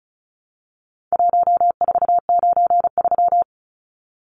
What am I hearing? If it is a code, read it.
Morse code, heard as a keyed tone, about 34 wpm, 700 Hz. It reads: 1493
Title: Numbers - Contest Style - 4-Digit 35wpm